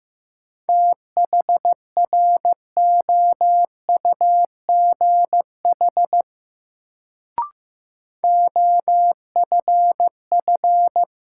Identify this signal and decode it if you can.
Morse code, heard as a keyed tone, about 15 wpm, 700 Hz. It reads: THROUGH OFF
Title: Sentences from Top 200 Words - New Word Drill - 15wpm